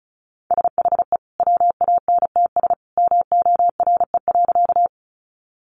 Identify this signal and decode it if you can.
Morse code, heard as a keyed tone, about 34 wpm, 700 Hz. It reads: SHE WANTS MORE.
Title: Easy Phrases 35wpm